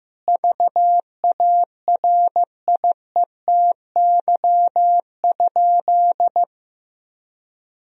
Morse code, heard as a keyed tone, about 15 wpm, 700 Hz. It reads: VARIETY?
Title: Sentences from Top 1000 Words - Review Part 4 - 15wpm